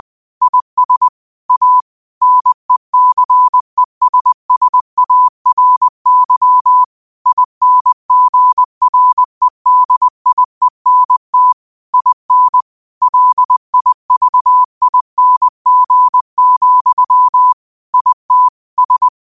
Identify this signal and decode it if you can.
Morse code, heard as a keyed tone, about 20 wpm, 1000 Hz. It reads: IS A NECESSARY INGREDIENT IN LIVING, IT S
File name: qotdfiles / 20wpm